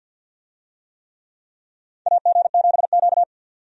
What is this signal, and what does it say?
Morse code, heard as a keyed tone, about 49 wpm, 700 Hz. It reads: AG7Q
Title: USA Call Signs - 4 Characters - 50wpm